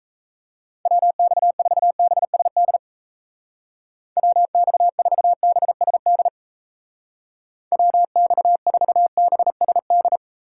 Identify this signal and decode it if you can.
Morse code, heard as a keyed tone, about 36 wpm, 700 Hz. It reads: WX4BSD WX4BSD WX4BSD